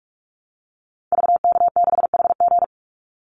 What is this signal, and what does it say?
Morse code, heard as a keyed tone, about 44 wpm, 700 Hz. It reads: VK6HG
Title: Call Signs Repeated 3x - 45wpm